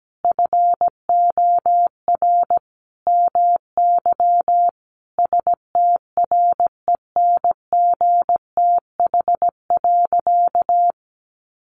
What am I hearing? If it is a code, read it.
Morse code, heard as a keyed tone, about 17 wpm, 700 Hz. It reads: FOR MY STRENGTH.
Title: Sentences from Top 1300 Words 17wpm